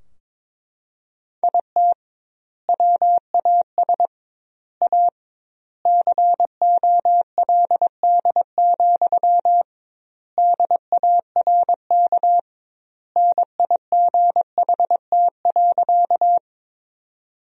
Morse code, heard as a keyed tone, about 22 wpm, 700 Hz. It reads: IT WAS A COLD, DARK NIGHT.
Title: 2x Word-Spacing - Sentences from Top 500 Words - 22wpm